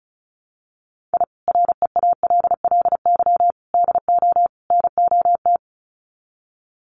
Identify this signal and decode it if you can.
Morse code, heard as a keyed tone, about 34 wpm, 700 Hz. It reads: I REALLY DO NOT
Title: Sets of 4 Words 35wpm